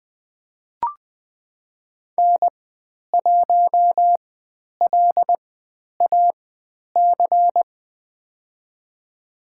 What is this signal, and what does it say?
Morse code, heard as a keyed tone, about 20 wpm, spaced as for 10 wpm, 700 Hz. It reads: N1LAC
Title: 102 Lesson 17g - Letter u - Call Signs 20wpm